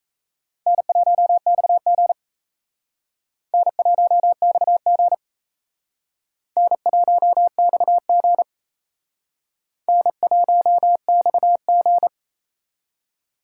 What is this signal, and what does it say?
Morse code, heard as a keyed tone, about 34 wpm, 700 Hz. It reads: N1XG N1XG N1XG N1XG
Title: Speed Racing - USA 1x2 Call Signs 28wpm